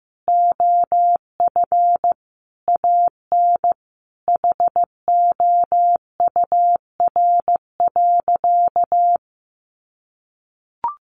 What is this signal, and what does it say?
Morse code, heard as a keyed tone, about 15 wpm, 700 Hz. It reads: OF AN HOUR.
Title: Sentences from Top 1300 Words 15wpm